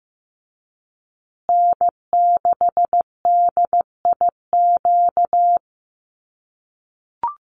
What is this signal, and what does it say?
Morse code, heard as a keyed tone, about 15 wpm, 700 Hz. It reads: N6DIQ
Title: USA Call Signs Repeated 3x 15wpm